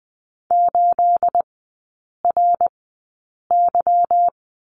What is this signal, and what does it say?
Morse code, heard as a keyed tone, about 20 wpm, spaced as for 8 wpm, 690 Hz. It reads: 8RY
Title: 212 Lesson 34e - Number 8 - Three Characters 20wpm